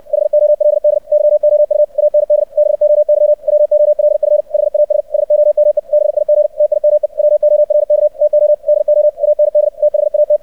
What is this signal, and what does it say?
Morse code, heard as a keyed tone, about 64 wpm, 600 Hz. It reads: HOLD YOU AND BOY WOLF HAS SOME BIG TIME WORK TO DO AND THAT
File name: archived / CRSnet